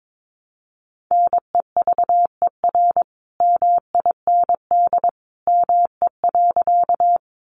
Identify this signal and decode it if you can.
Morse code, heard as a keyed tone, about 22 wpm, 700 Hz. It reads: NEVER MIND ME.
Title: Sentences from Top 500 Words 22wpm